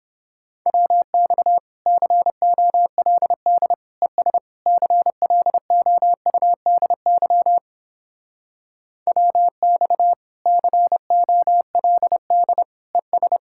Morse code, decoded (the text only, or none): WX COLD ES CLOUDY WX COLD ES